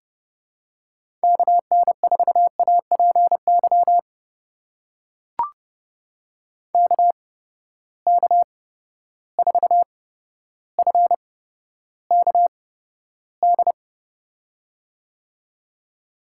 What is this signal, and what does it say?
Morse code, heard as a keyed tone, about 31 wpm, spaced as for 13 wpm, 700 Hz. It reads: KN4APY KK4FKD
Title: USA Call Signs - 6 Characters - 8x Character Spacing - 30wpm